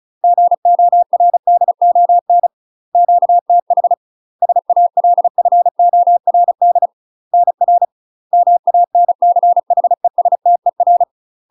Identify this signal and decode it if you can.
Morse code, heard as a keyed tone, about 35 wpm, 700 Hz. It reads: GORDON QTH SALFORD NR MANCHESTER
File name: UK MASTER QSOs TRAINING / 35WPM / UK MASTER QSOs 0013